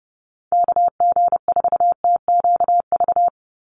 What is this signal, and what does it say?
Morse code, heard as a keyed tone, about 30 wpm, 700 Hz. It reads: KG4TQV
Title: USA Call Signs 30wpm